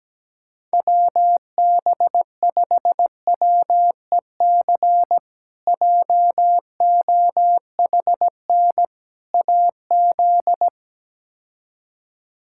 Morse code, heard as a keyed tone, about 17 wpm, 700 Hz. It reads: WB5WEC JOHN AZ
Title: K1USN SST QSO Practice - 17wpm